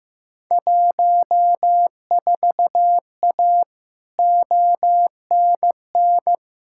Morse code, heard as a keyed tone, about 15 wpm, 700 Hz. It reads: 14A ONN
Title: Field Day Exchanges Encore - 15wpm